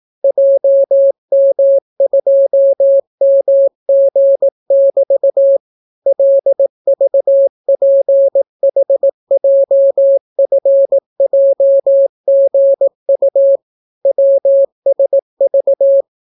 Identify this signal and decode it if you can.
Morse code, heard as a keyed tone, about 18 wpm, 550 Hz. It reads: JM2MG<BT> LVPHJFJGU WSV